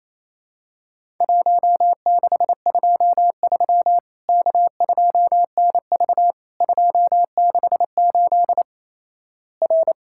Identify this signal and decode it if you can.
Morse code, heard as a keyed tone, about 28 wpm, 700 Hz. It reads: 1623 K2NV 268 R
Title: Worked All Europe DX Contest - QTC Exchange - 28wpm